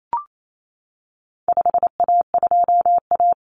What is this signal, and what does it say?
Morse code, heard as a keyed tone, about 28 wpm, 700 Hz. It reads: 5A2A
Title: Common Cut Numbers - 4-Digit 28wpm